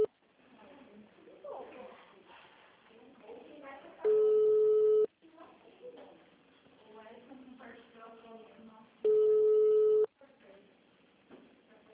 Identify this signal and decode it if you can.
Ringback tone (Europe, 425 Hz), with speech.
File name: out- -2010- - - .545